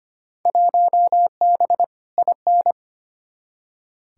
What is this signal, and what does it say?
Morse code, heard as a keyed tone, about 25 wpm, 700 Hz. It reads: 1B IN
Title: Field Day Exchanges - 25wpm